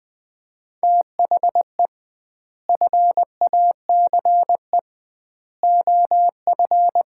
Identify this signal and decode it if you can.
Morse code, heard as a keyed tone, about 20 wpm, 700 Hz. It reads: THE FACE OF
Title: 2x Word-Spacing - Sets of 3 Words - 20wpm